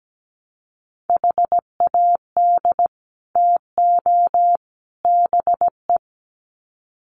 Morse code, heard as a keyed tone, about 17 wpm, 700 Hz. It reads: HAD TO BE
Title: Sets of 3 Words 17wpm